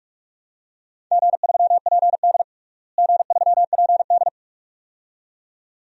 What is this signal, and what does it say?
Morse code, heard as a keyed tone, about 46 wpm, 700 Hz. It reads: G3PD G3PD
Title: DXCC-100 Call Signs - 4 Characters - Repeated 2x - 45wpm